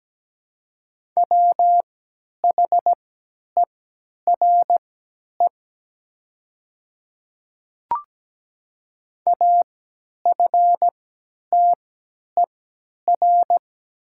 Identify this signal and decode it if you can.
Morse code, heard as a keyed tone, about 17 wpm, spaced as for 10 wpm, 700 Hz. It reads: WHERE AFTER
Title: 3x Character Spacing - Top 200 Words - 17wpm